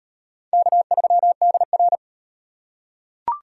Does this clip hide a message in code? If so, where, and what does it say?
Yes: Morse code from 0.5 to 2.0 s, heard as a keyed tone, about 33 wpm, 700 Hz. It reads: K3DR